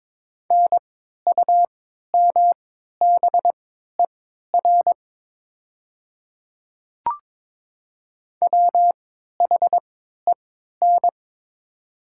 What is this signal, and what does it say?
Morse code, heard as a keyed tone, about 22 wpm, spaced as for 13 wpm, 700 Hz. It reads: NUMBER WHEN
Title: 3x Character Spacing - Top 100 Words - 22wpm